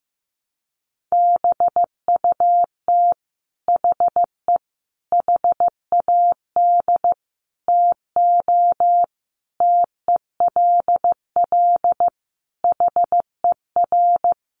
Morse code, heard as a keyed tone, about 15 wpm, 700 Hz. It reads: BUT HE HAD TO TELL HER
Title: Sentences from Top 800 Words 15wpm